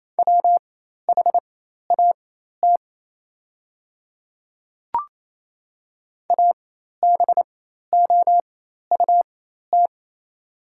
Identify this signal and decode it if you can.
Morse code, heard as a keyed tone, about 28 wpm, spaced as for 13 wpm, 700 Hz. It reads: WHAT ABOUT
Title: 4x Character Spacing - Top 100 Words - 28wpm